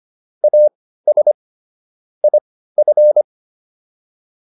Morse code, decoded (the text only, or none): AS IF